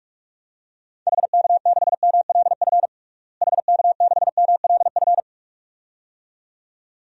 Morse code, heard as a keyed tone, about 46 wpm, 700 Hz. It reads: HK6MLF HK6MLF
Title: DXCC-100 Call Signs - 6 Characters - Repeated 2x - 45wpm